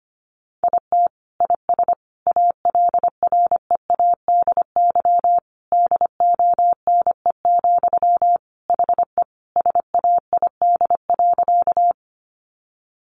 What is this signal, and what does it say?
Morse code, heard as a keyed tone, about 25 wpm, 700 Hz. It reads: IT IS ALREADY DONE, HE SAID.
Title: Sentences from Top 700 Words 25wpm